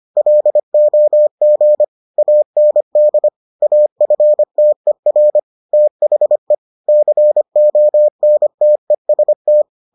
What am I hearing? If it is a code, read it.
Morse code, heard as a keyed tone, about 25 wpm, 600 Hz. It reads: LOG AND AFTER THE CONTEST